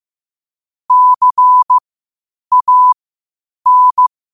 Morse code, heard as a keyed tone, about 15 wpm, spaced as for 8 wpm, 1000 Hz. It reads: CAN
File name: qotdfiles / 8wpm